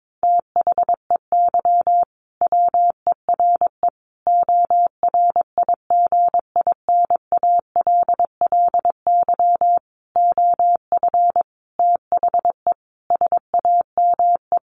Morse code, heard as a keyed tone, about 22 wpm, 700 Hz. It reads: THEY WERE ORIGINALLY OF THE SAME
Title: Sentences from Top 1500 Words 22wpm